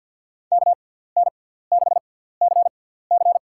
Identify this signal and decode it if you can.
Morse code, heard as a keyed tone, about 50 wpm, spaced as for 17 wpm, 700 Hz. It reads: KN6CC
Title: USA Call Signs - 5 Characters - 6x Character Spacing - 50wpm